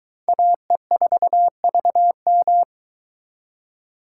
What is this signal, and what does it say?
Morse code, heard as a keyed tone, about 22 wpm, 700 Hz. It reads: AE4VM